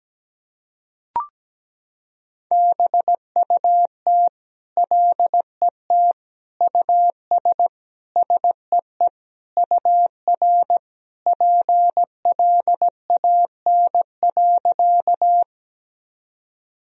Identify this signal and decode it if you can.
Morse code, heard as a keyed tone, about 17 wpm, 700 Hz. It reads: BUT LET US SEE UR PLAN.
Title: Sentences from Top 500 Words and Abbreviations 17wpm